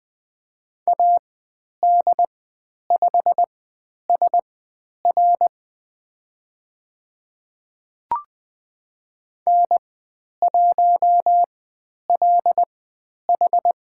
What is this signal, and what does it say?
Morse code, heard as a keyed tone, about 20 wpm, spaced as for 10 wpm, 700 Hz. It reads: AD5SR N1LH
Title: 102 Lesson 17g - Letter u - Call Signs 20wpm